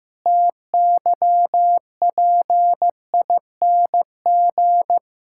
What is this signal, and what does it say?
Morse code, heard as a keyed tone, about 16 wpm, 700 Hz. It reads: TYPING
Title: Pangram Challenge v2 - 15wpm